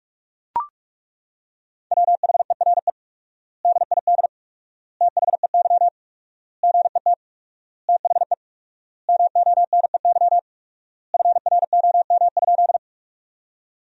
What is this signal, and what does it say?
Morse code, heard as a keyed tone, about 42 wpm, 700 Hz. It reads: WHERE DID THEY GET THE MONEY FROM?
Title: 4x Word-Spacing - Sentences from Top 400 Words - 45wpm